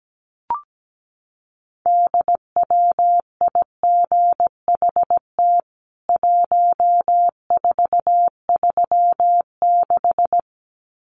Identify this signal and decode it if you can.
Morse code, heard as a keyed tone, about 17 wpm, 700 Hz. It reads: DWIGHT 1436
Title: CWT Exchange 17wpm